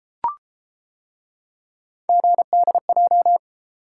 Morse code, heard as a keyed tone, about 31 wpm, 700 Hz. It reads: GDJ